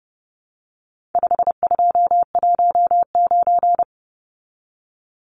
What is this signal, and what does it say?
Morse code, heard as a keyed tone, about 30 wpm, 700 Hz. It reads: 5219
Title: Numbers - 4-Digit 30wpm